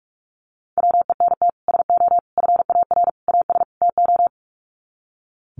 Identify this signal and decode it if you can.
Morse code, heard as a keyed tone, about 44 wpm, 700 Hz. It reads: WENT SO FAR AS TO
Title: Sets of 5 Words Encore 45wpm